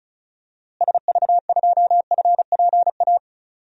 Morse code, heard as a keyed tone, about 34 wpm, 700 Hz. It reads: SV2FPA
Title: DXCC-100 Call Signs - 6 Characters - Repeated 3x - 35wpm